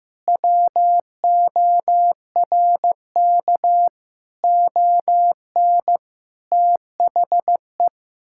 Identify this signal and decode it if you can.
Morse code, heard as a keyed tone, about 15 wpm, 700 Hz. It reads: WORK ON THE
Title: Sentences from Top 1000 Words - Review Part 5 - 15wpm